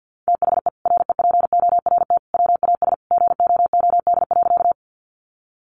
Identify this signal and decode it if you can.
Morse code, heard as a keyed tone, about 50 wpm, 700 Hz. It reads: THE REPORT WAS GOOD.
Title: Sentences from Top 900 Words 50wpm